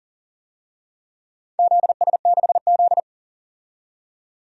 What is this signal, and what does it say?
Morse code, heard as a keyed tone, about 40 wpm, 700 Hz. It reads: ZS6Z
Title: DXCC-100 Call Signs - 4 Characters - 40wpm